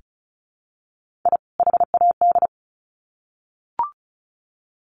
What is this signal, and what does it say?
Morse code, heard as a keyed tone, about 34 wpm, 700 Hz. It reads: I HAD
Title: Sets of 2 Words 35wpm